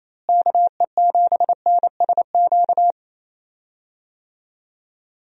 Morse code, heard as a keyed tone, about 28 wpm, 700 Hz. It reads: KE7NSQ
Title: USA Call Signs - 6 Characters - Repeated 3x - 28wpm